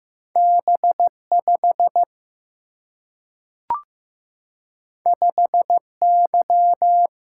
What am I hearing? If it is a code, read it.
Morse code, heard as a keyed tone, about 15 wpm, 700 Hz. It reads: B 5 5 Y
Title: Single-Letter Single-Number 15wpm